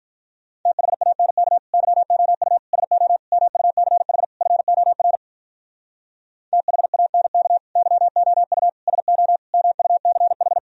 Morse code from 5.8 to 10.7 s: THANK YOU SO MUCH

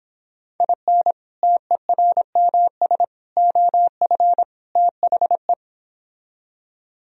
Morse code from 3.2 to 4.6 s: OF

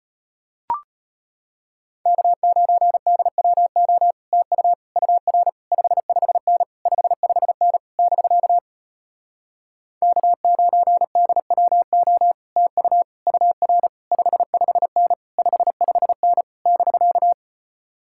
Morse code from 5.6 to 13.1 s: 55N 55N <BK> K9DWO TU

